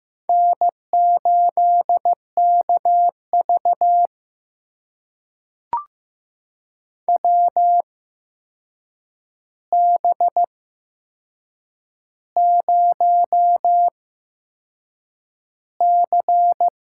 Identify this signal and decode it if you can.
Morse code, heard as a keyed tone, about 15 wpm, spaced as for 6 wpm, 700 Hz. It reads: N8KV WB0C
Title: USA Call Signs - 4 Characters - 8x Character Spacing - 15wpm